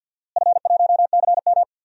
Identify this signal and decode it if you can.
Morse code, heard as a keyed tone, about 52 wpm, 700 Hz. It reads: F1CM
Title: DXCC-100 Call Signs - 4 Characters - Repeated 2x - 50wpm